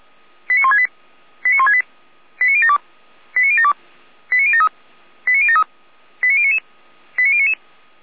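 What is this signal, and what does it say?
Five-tone selective call (ZVEI), heard as quick runs of five tones, tones of about 70 ms. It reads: 89168 89168 89072 89072 89073 89073 89099 89099
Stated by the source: ZVEI five-tone selective call received by SDR.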